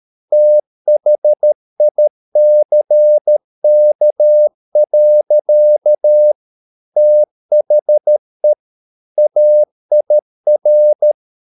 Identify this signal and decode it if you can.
Morse code, heard as a keyed tone, about 13 wpm, 600 Hz. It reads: THICK. THE AIR
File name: Never 0013wpm